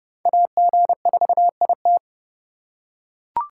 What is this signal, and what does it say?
Morse code, heard as a keyed tone, about 30 wpm, 700 Hz. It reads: AG4IT